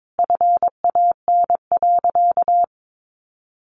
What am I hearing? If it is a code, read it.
Morse code, heard as a keyed tone, about 22 wpm, 700 Hz. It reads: FAN.
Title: Sentences from Top 1300 Words 22wpm